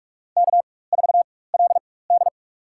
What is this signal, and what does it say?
Morse code, heard as a keyed tone, about 44 wpm, spaced as for 21 wpm, 690 Hz. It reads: K4LD
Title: USA Call Signs - 4x Character Spacing - 45wpm